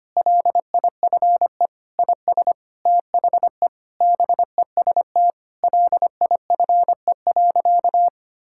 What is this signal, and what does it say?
Morse code, heard as a keyed tone, about 25 wpm, 700 Hz. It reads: LIFE IS THE BEST LIFE.
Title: Sentences from Top 1000 Words - Review Part 6 - 25wpm